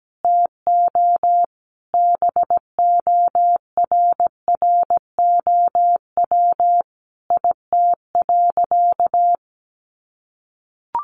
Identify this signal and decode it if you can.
Morse code, heard as a keyed tone, about 17 wpm, 700 Hz. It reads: TO BORROW IT.
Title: Sentences from Top 1500 Words 17wpm